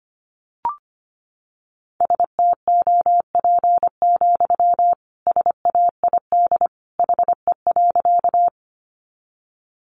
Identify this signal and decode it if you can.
Morse code, heard as a keyed tone, about 25 wpm, 700 Hz. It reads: STOP, SAID HE.
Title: Sentences from Top 300 Words 25wpm